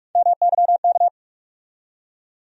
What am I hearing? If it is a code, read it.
Morse code, heard as a keyed tone, about 42 wpm, 700 Hz. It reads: MYK
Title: USA Call Sign Postfixes - 3 Letters 45wpm